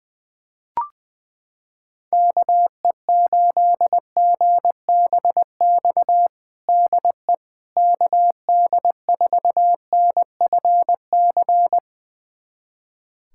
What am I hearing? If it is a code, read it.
Morse code, heard as a keyed tone, about 20 wpm, 700 Hz. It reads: KE8GBX DE KD4NFC
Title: USA Callsign DE Callsign - 6 Characters - 20wpm